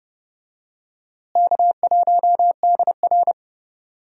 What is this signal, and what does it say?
Morse code, heard as a keyed tone, about 32 wpm, 700 Hz. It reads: K1DR
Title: USA Call Signs - 4 Characters - Repeated 2x - 30wpm